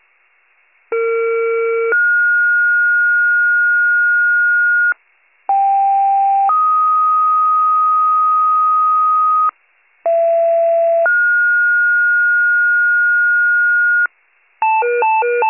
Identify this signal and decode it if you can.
Two-tone page, heard as a one-second tone, then a three-second tone, about 460 then 1430 Hz, about 790 then 1250 Hz, about 670 then 1430 Hz, with noise. It is heard from 0.9 to 14.1 s.